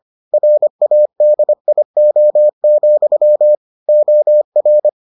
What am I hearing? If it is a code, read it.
Morse code, heard as a keyed tone, about 25 wpm, 600 Hz. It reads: RADIO, OR